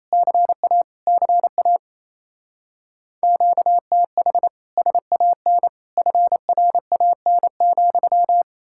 Morse code, heard as a keyed tone, about 36 wpm, 700 Hz. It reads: CA CA QTH SAN FRAN,